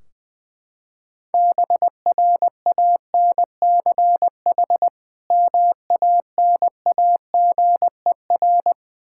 Morse code, heard as a keyed tone, about 20 wpm, 700 Hz. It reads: BRANCH MANAGER
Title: Practice QSO - Exchange 3 - More Info - 20wpm